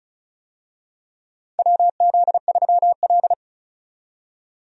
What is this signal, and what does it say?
Morse code, heard as a keyed tone, about 36 wpm, 700 Hz. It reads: WZ3L
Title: USA Call Signs - 4 Characters - 35wpm